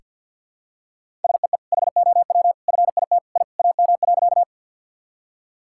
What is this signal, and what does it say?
Morse code, heard as a keyed tone, about 48 wpm, 700 Hz. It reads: SEE HOW FIT I AM.
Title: Sentences from Top 1000 Words - Review Part 5 - 50wpm